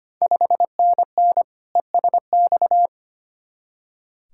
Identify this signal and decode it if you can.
Morse code, heard as a keyed tone, about 25 wpm, 700 Hz. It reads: 5NN ESX
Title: Ontario QSO Party Exchange 25wpm